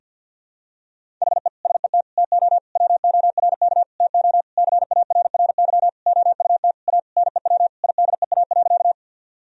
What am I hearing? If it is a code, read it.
Morse code, heard as a keyed tone, about 50 wpm, 700 Hz. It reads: HE SET TO WORK TO CARRY OUT A NEW IDEA.
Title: Sentences from Top 1000 Words - Review Part 1 - 50wpm